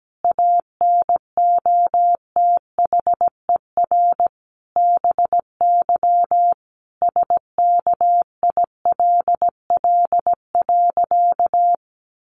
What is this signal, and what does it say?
Morse code, heard as a keyed tone, about 17 wpm, 700 Hz. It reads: ANOTHER BY SKILL.
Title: Sentences from Top 1200 Words 17wpm